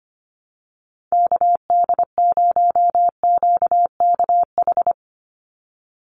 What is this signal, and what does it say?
Morse code, heard as a keyed tone, about 25 wpm, 700 Hz. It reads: KD0QKH
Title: USA Call Signs 25wpm